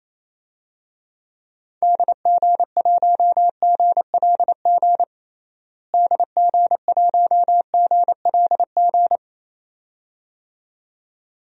Morse code, heard as a keyed tone, about 28 wpm, 700 Hz. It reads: DG1GLG DG1GLG
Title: DXCC-100 Call Signs - 6 Characters - Repeated 2x - 28wpm